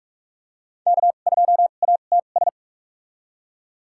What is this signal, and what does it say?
Morse code, heard as a keyed tone, about 41 wpm, spaced as for 31 wpm, 700 Hz. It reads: K2ATS
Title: USA Call Signs - 5 Characters - 2x Character Spacing - 45wpm